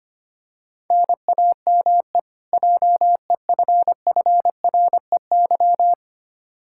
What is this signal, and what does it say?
Morse code, heard as a keyed tone, about 25 wpm, 700 Hz. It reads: NAME JEFFREY
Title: Practice QSO - Exchange 1 - Beginning - 25wpm